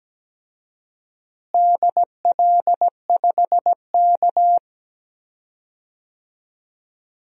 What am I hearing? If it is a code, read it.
Morse code, heard as a keyed tone, about 17 wpm, 700 Hz. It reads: DL5K
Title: DXCC-100 Call Signs - 4 Characters - Repeated 2x - 17wpm